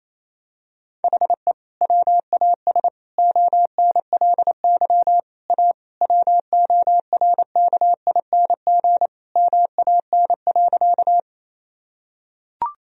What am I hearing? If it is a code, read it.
Morse code, heard as a keyed tone, about 28 wpm, 700 Hz. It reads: HE WAS ONLY A WORKING MAN.
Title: Sentences from Top 1000 Words - Review Part 4 - 28wpm